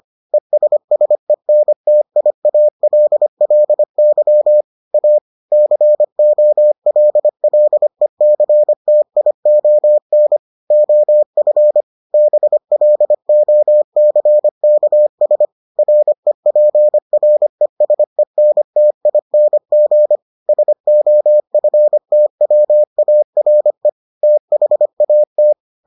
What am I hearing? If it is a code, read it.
Morse code, heard as a keyed tone, about 25 wpm, 600 Hz. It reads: ESSENTIALLY A COLLECTION OF BLOCKS REPRESENTING SOFTWARE THAT